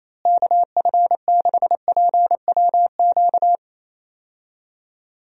Morse code, heard as a keyed tone, about 27 wpm, 700 Hz. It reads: KF6PWQ
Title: USA Call Signs - 6 Characters - Repeated 3x - 28wpm